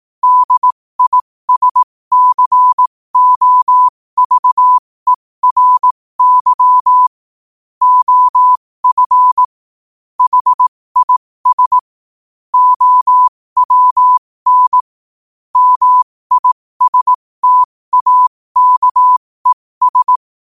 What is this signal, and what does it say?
Morse code, heard as a keyed tone, about 18 wpm, spaced as for 15 wpm, 1000 Hz. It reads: DISCOVERY OF HIS OWN MISTAKES
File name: qotdfiles / 15wpm